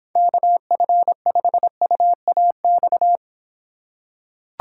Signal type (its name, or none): Morse code